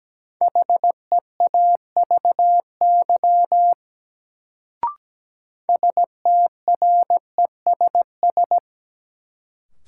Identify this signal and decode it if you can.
Morse code, heard as a keyed tone, about 17 wpm, 700 Hz. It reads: HEAVY STRESS
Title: Sentences from Top 1000 Words - Review Part 6 - New Word Drill - 17wpm